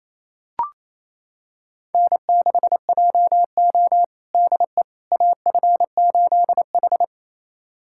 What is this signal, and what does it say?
Morse code, heard as a keyed tone, about 28 wpm, 700 Hz. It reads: N6JO DE AF8H
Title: USA Callsign DE Callsign - 4 Characters - 28wpm